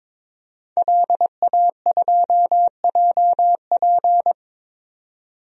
Morse code, heard as a keyed tone, about 22 wpm, 700 Hz. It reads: LA2JP
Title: DXCC-100 Call Signs - 5 Characters - 22wpm